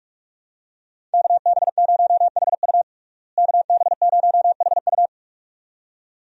Morse code, heard as a keyed tone, about 44 wpm, 700 Hz. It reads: KB0HU KB0HU
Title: USA Call Signs - 5 Characters - Repeated 2x - 45wpm